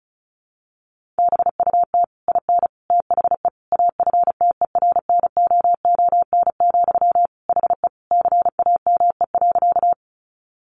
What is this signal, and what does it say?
Morse code, heard as a keyed tone, about 34 wpm, 700 Hz. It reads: BUT IN THE AFTERNOON, HE CAME.
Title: Sentences from Top 1500 Words 35wpm